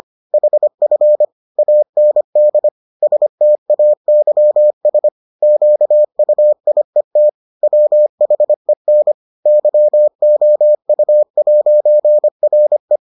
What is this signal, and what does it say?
Morse code, heard as a keyed tone, about 25 wpm, 600 Hz. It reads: HF AND STAYS QUIET WHEN YOU'RE